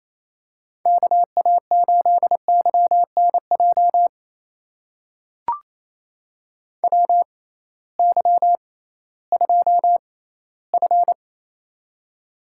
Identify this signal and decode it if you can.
Morse code, heard as a keyed tone, about 28 wpm, spaced as for 16 wpm, 700 Hz. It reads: KA8YNJ WY2F
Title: USA Call Signs - 6x Character Spacing - 28wpm